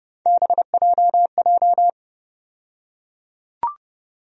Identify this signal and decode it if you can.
Morse code, heard as a keyed tone, about 30 wpm, 700 Hz. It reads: BJJ
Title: USA Call Sign Postfixes - 3 Letters 30wpm